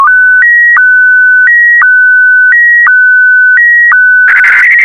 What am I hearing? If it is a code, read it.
SSTV (slow-scan television), Scottie DX.